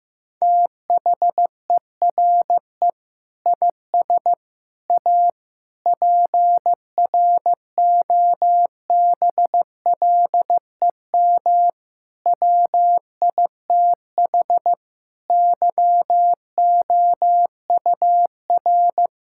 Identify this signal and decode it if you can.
Morse code, heard as a keyed tone, about 15 wpm, 700 Hz. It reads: THERE IS A PROBLEM WITH YOUR
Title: Sentences from Top 1000 Words - Review Part 5 - 15wpm